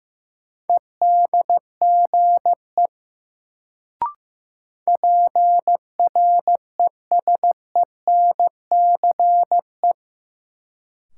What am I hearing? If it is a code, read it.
Morse code, heard as a keyed tone, about 16 wpm, 700 Hz. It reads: EDGE PRESENCE
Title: Sets of 3 Words - New Word Drill - 15wpm